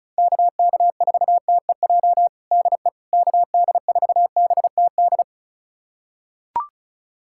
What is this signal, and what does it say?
Morse code, heard as a keyed tone, about 36 wpm, 700 Hz. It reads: KK4TEJ DE KD4BTD
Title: USA Callsign DE Callsign - 6 Characters - 35wpm